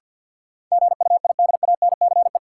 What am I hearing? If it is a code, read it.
Morse code, heard as a keyed tone, about 53 wpm, 700 Hz. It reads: GUIDANCE
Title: Sentences from Top 1600 Words - New Word Drill Alt - 50wpm